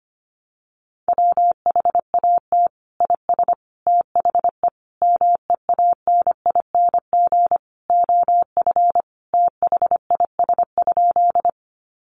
Morse code, heard as a keyed tone, about 25 wpm, 700 Hz. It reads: WHAT IS THE MEANING OF THIS?
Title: Sentences from Top 900 Words 25wpm